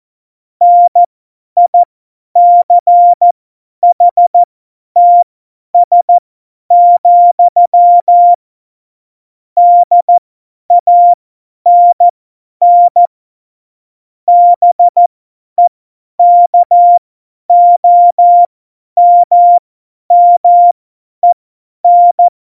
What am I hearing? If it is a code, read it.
Morse code, heard as a keyed tone, about 14 wpm, spaced as for 10 wpm, 700 Hz. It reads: NICHTS, DANN BEKOMMEN
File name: Morsekurs / Witze / Witz 12 Mai 2020, 50 F70